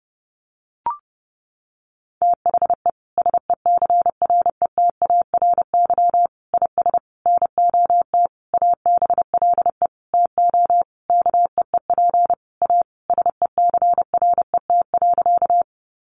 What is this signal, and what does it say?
Morse code, heard as a keyed tone, about 30 wpm, 700 Hz. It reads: THE SECRETARY IS NOT ABLE TO KEEP A SECRET.
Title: Sentences from Top 1300 Words 30wpm